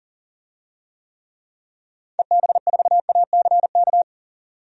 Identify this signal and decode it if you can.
Morse code, heard as a keyed tone, about 37 wpm, 700 Hz. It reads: EB4ACK
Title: DXCC-100 Call Signs - 6 Characters - 40wpm